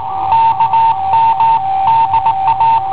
Morse code, heard as a keyed tone, about 19 wpm, 910 Hz. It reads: KM DA
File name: i7ohp / cassetta i7ohp B